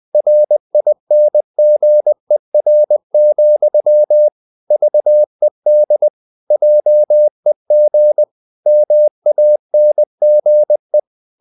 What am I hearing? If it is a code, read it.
Morse code, heard as a keyed tone, about 20 wpm, 600 Hz. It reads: RINGER, VED JEG MANGE
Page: Ferie 20 wpm